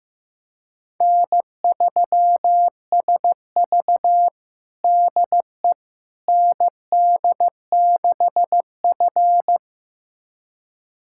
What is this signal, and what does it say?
Morse code, heard as a keyed tone, about 15 wpm, 700 Hz. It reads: N3SV DE ND6F
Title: USA Callsign DE Callsign - 4 Characters - 15wpm